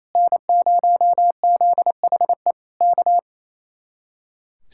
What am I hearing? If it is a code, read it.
Morse code, heard as a keyed tone, about 26 wpm, 700 Hz. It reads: N0ZHE K
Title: Speed-Racing Common QSO Phrases - 28wpm